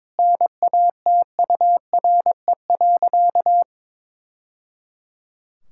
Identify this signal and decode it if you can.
Morse code, heard as a keyed tone, about 22 wpm, 700 Hz. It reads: NATURE.
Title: Sentences from Top 1000 Words - Review Part 6 - 22wpm